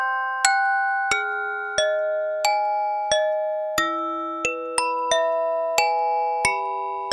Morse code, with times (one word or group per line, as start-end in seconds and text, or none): none